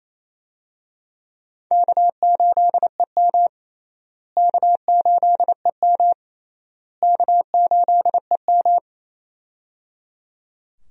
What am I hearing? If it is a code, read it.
Morse code, heard as a keyed tone, about 28 wpm, 700 Hz. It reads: K8EM K8EM K8EM
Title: USA Call Signs - 4 Characters - Repeated 3x - 28wpm